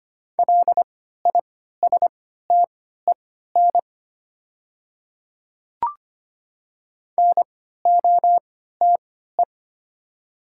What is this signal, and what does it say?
Morse code, heard as a keyed tone, about 25 wpm, spaced as for 14 wpm, 700 Hz. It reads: LISTEN NOTE
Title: 3x Character Spacing - Top 500 Words - 25wpm